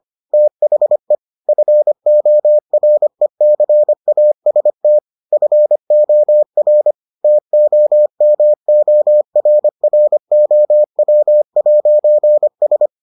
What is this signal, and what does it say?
Morse code, heard as a keyed tone, about 25 wpm, 600 Hz. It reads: THE FORECAST FOR TOMORROW'S